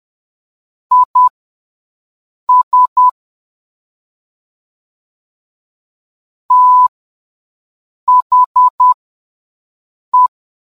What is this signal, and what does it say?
Morse code, heard as a keyed tone, about 10 wpm, spaced as for 5 wpm, 1000 Hz. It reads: IS THE